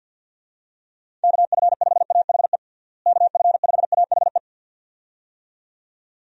Morse code, heard as a keyed tone, about 51 wpm, 700 Hz. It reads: KF5AHE KF5AHE
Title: USA Call Signs - 6 Characters - Repeated 2x - 50wpm